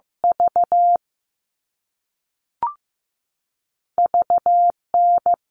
Morse code, heard as a keyed tone, about 15 wpm, 700 Hz. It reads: V VN
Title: All Cut Numbers - 2-Digit 15wpm